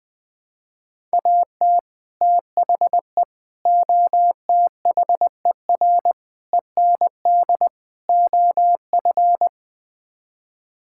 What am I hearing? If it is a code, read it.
Morse code, heard as a keyed tone, about 20 wpm, 700 Hz. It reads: AT THE OTHER END OF
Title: Build A Phrase - 5-Word Phrases - 20wpm